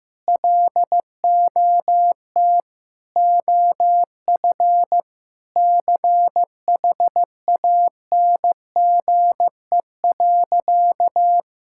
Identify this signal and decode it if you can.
Morse code, heard as a keyed tone, about 16 wpm, 700 Hz. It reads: LOT OF CHANGE.
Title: Sentences from Top 1000 Words - Review Part 5 - 15wpm